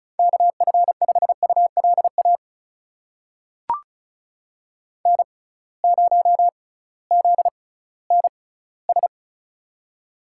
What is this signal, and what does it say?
Morse code, heard as a keyed tone, about 36 wpm, spaced as for 19 wpm, 700 Hz. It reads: KF5ULA N0ZNS
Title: USA Call Signs - 6x Character Spacing - 35wpm